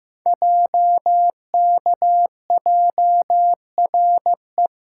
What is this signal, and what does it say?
Morse code, heard as a keyed tone, about 15 wpm, 700 Hz. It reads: JKJRE
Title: Code Groups - 5 Letters - 15wpm